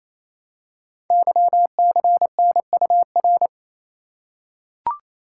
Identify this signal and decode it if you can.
Morse code, heard as a keyed tone, about 27 wpm, 700 Hz. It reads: YCNUR
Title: Code Groups - 5 Letter:Numbers - 28wpm